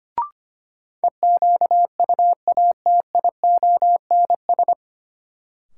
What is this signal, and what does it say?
Morse code, heard as a keyed tone, about 25 wpm, 700 Hz. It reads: EQUATIONS
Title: Sets of 5 Words - New Word Drill - 25wpm